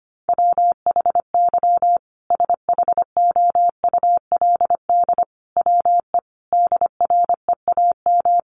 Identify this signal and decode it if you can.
Morse code, heard as a keyed tone, about 25 wpm, 700 Hz. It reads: WHY SHOULD WE DREAM
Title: Sentences from Top 1500 Words 25wpm